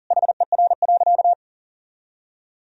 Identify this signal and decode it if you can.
Morse code, heard as a keyed tone, about 40 wpm, 700 Hz. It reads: HER.
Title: Sentences from Top 1000 Words - Review Part 1 - 40wpm